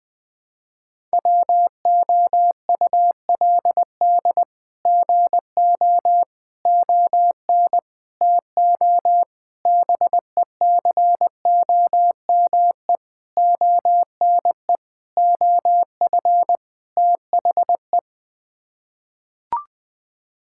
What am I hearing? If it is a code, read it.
Morse code, heard as a keyed tone, about 20 wpm, 700 Hz. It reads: WOULD GO ON TO BECOME ONE OF THE
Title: Sets of 8 Words - 20wpm